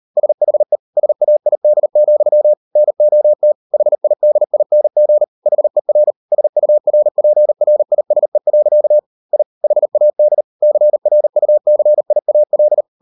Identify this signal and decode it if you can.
Morse code, heard as a keyed tone, about 40 wpm, 600 Hz. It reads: SHE SAID, NOT HIDING HER SURPRISE. I HAD CRUCIAL